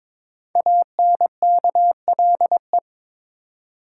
Morse code, heard as a keyed tone, about 23 wpm, 700 Hz. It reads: ANKLE
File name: 5 Letter Words - First 3 Letters Given - 22wpm